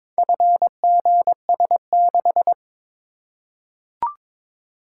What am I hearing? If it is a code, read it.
Morse code, heard as a keyed tone, about 22 wpm, 700 Hz. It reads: FGS6
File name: practice / 22wpm / Four Letter-Number Combinations - 22wpm